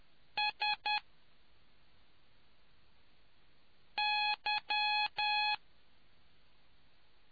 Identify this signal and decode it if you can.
Morse code, heard as a keyed tone, about 10 wpm, 840 Hz. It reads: S Y